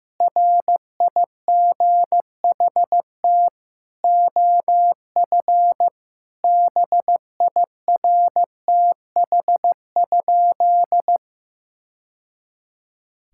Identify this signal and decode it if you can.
Morse code, heard as a keyed tone, about 15 wpm, 700 Hz. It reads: RIGHT OF BIRTH?
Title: Sentences from Top 1000 Words - Review Part 6 - 15wpm